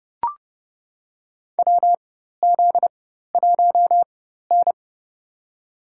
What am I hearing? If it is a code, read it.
Morse code, heard as a keyed tone, about 30 wpm, spaced as for 14 wpm, 700 Hz. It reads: WZ1N
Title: USA Call Signs - 4 Characters - 4x Character Spacing - 30wpm